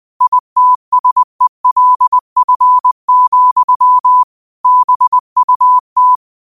Morse code, heard as a keyed tone, about 20 wpm, 1000 Hz. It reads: ITSELF, BUT
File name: qotdfiles / 20wpm